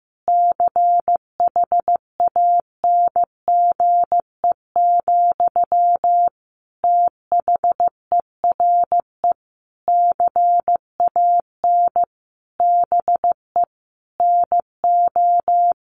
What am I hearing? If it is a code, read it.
Morse code, heard as a keyed tone, about 14 wpm, 700 Hz. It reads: CHANGE, THERE CAN BE NO
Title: Sentences from Top 1400 Words 15wpm